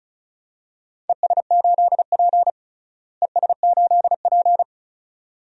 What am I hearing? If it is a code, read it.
Morse code, heard as a keyed tone, about 36 wpm, 700 Hz. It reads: ES8P ES8P
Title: DXCC-100 Call Signs - 4 Characters - Repeated 2x - 35wpm